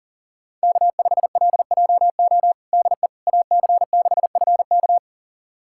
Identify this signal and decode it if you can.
Morse code, heard as a keyed tone, about 40 wpm, 700 Hz. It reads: K5LJO DE AC6FK
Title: USA Callsign DE Callsign - 5 Characters - 40wpm